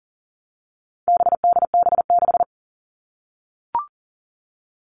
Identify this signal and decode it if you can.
Morse code, heard as a keyed tone, about 40 wpm, 700 Hz. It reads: BDB6
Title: All Cut Numbers - 4-Digit 40wpm